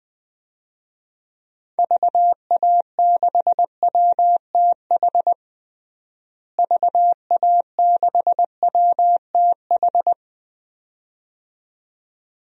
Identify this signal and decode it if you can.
Morse code, heard as a keyed tone, about 21 wpm, 700 Hz. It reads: VA6WTH VA6WTH
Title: DXCC-100 Call Signs - 6 Characters - Repeated 2x - 20wpm